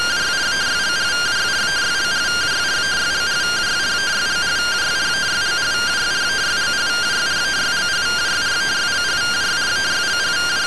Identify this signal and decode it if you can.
Two-tone teleprinter signal, about 170 Hz shift.
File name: findings / CT AXIAL -377.gen-13 rtty rx